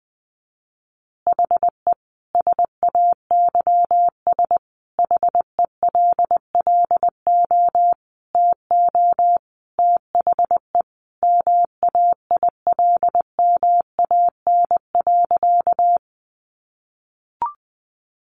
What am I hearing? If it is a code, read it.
Morse code, heard as a keyed tone, about 20 wpm, 700 Hz. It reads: HE SAYS HELLO TO THE MAILMAN.
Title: Additional Easy Phrases 20wpm